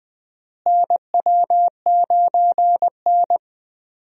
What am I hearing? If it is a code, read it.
Morse code, heard as a keyed tone, about 20 wpm, 700 Hz. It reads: NW9N
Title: USA Call Signs - 4 Characters - Repeated 3x - 20wpm